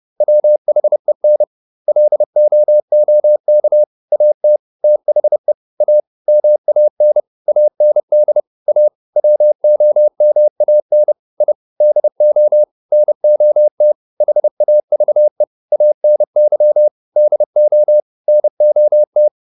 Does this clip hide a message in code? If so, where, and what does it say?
Yes: Morse code from 0.2 to 19.3 s, heard as a keyed tone, about 30 wpm, 600 Hz. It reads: WHEN LOOK AT THE A MAN AND A WOMAN I DO NOT HAVE ANY DO NOT